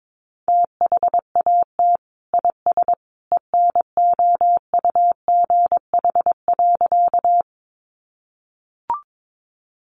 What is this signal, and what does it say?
Morse code, heard as a keyed tone, about 22 wpm, 700 Hz. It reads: THAT IS ENOUGH.
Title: Sentences from Top 400 Words 22wpm